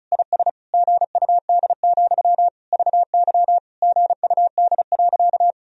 Morse code, heard as a keyed tone, about 34 wpm, 700 Hz. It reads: IS GUD, VY GUD.
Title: Sentences from Top 200 Words and Abbreviations 35wpm